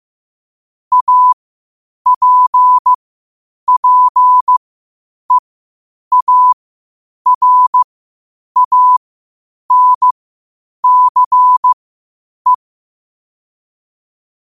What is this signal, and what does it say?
Morse code, heard as a keyed tone, about 15 wpm, spaced as for 8 wpm, 1000 Hz. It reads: APPEARANCE
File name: qotdfiles / 8wpm